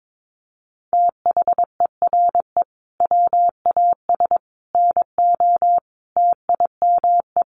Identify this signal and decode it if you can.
Morse code, heard as a keyed tone, about 22 wpm, 700 Hz. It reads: THERE WAS NO TIME
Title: Sentences from Top 400 Words 22wpm